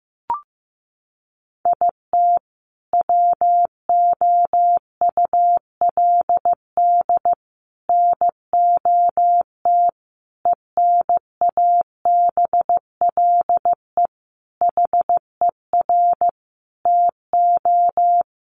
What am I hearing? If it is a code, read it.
Morse code, heard as a keyed tone, about 15 wpm, 700 Hz. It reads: IT WOULD NOT ENABLE HER TO
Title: Sentences from Top 1500 Words 15wpm